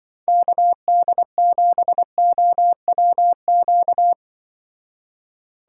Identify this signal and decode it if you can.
Morse code, heard as a keyed tone, about 24 wpm, 700 Hz. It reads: KD7OWQ
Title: Speed Racing - USA 2x3 Call Signs 20wpm